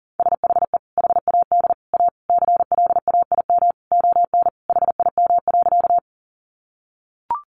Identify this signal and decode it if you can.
Morse code, heard as a keyed tone, about 40 wpm, 700 Hz. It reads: SHE HAD A CLAIM ON HIM.
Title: Sentences from Top 1500 Words 40wpm